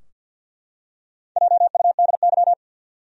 Morse code, heard as a keyed tone, about 52 wpm, 700 Hz. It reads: JUDY
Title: USA Common Names - Max 4 Letters - 50wpm